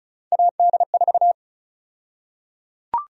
Morse code, heard as a keyed tone, about 34 wpm, 700 Hz. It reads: AD4
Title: USA Call Sign Prefixes - 35wpm